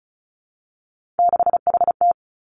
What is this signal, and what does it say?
Morse code, heard as a keyed tone, about 36 wpm, 700 Hz. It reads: 6HT
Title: Three Challenging Letter-Number Combinations 35wpm